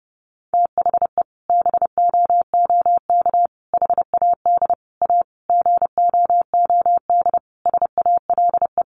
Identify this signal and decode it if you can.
Morse code, heard as a keyed tone, about 30 wpm, 700 Hz. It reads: THE BOOK HAD A GOOD SALE
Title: Sentences from Top 1400 Words 30wpm